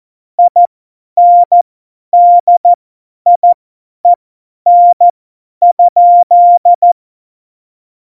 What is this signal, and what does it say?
Morse code, heard as a keyed tone, about 14 wpm, spaced as for 10 wpm, 700 Hz. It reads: INDIEN?
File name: Morsekurs / Witze / Witz 15. August 17, Tempo 50 F70